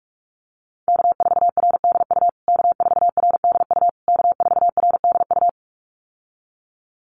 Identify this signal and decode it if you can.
Morse code, heard as a keyed tone, about 44 wpm, 700 Hz. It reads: K4RDU K4RDU K4RDU
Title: USA Call Signs Repeated 3x 45wpm